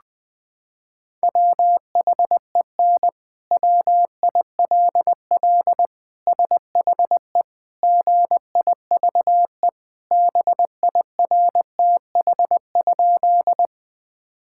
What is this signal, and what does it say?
Morse code, heard as a keyed tone, about 21 wpm, 700 Hz. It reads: WHEN WILL SHE GIVE BIRTH?
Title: Sentences from Top 1000 Words - Review Part 5 - 20wpm